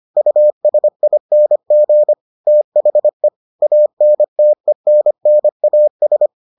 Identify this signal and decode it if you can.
Morse code, heard as a keyed tone, about 25 wpm, 600 Hz. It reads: USING THE ANTENNAS